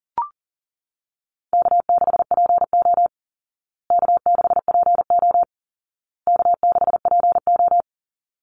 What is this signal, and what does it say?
Morse code, heard as a keyed tone, about 40 wpm, 700 Hz. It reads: K6PO K6PO K6PO
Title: Call Signs Repeated 2x - 40wpm